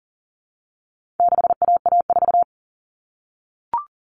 Morse code, heard as a keyed tone, about 40 wpm, 700 Hz. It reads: 6AA4
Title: All Cut Numbers - 4-Digit 40wpm